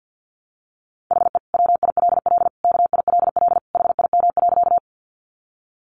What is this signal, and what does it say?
Morse code, heard as a keyed tone, about 50 wpm, 700 Hz. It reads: HE WILL KILL HIM.
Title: Sentences from Top 1400 Words 50wpm